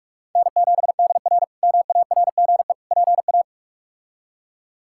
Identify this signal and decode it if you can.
Morse code, heard as a keyed tone, about 46 wpm, 700 Hz. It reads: N7DR MARGE PA
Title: K1USN SST QSO Practice - 45wpm